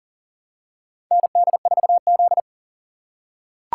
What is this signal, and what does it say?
Morse code, heard as a keyed tone, about 40 wpm, 700 Hz. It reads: ND4Z
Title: USA Call Signs - 4 Characters - 2x Character Spacing - 40wpm